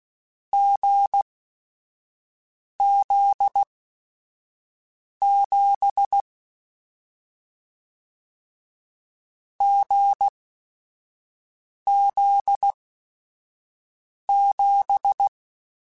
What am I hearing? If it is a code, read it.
Morse code, heard as a keyed tone, about 16 wpm, spaced as for 5 wpm, 800 Hz. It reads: GZ7 GZ7